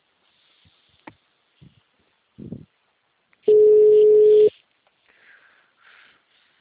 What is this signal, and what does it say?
Ringback tone (Europe, 425 Hz).
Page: 191.252.64.14 Capital - SP